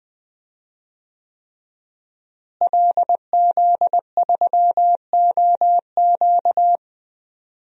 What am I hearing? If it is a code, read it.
Morse code, heard as a keyed tone, about 20 wpm, 700 Hz. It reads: LZ3OQ
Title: DXCC-100 Call Signs - 5 Characters - Repeated 3x - 20wpm